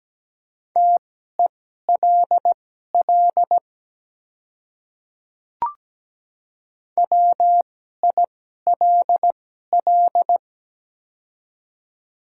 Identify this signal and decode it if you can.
Morse code, heard as a keyed tone, about 17 wpm, spaced as for 12 wpm, 690 Hz. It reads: TELL WILL
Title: 2x Character Spacing - Top 300 Words - 17wpm